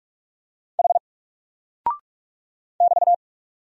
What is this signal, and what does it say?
Morse code, heard as a keyed tone, about 44 wpm, 700 Hz. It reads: H <BT>
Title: Single Character (Letter:Number:Punctuation:Prosign) Alt 45wpm